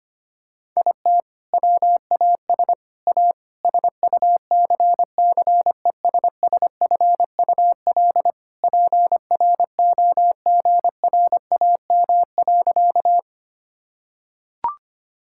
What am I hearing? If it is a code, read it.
Morse code, heard as a keyed tone, about 25 wpm, 700 Hz. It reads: IT WAS A SUCCESSFUL PROGRAM.
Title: Sentences from Top 1000 Words - Review Part 5 - 25wpm